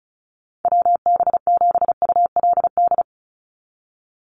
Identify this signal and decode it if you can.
Morse code, heard as a keyed tone, about 34 wpm, 700 Hz. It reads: WB7ULD
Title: NA SOTA Chaser Callsigns 35wpm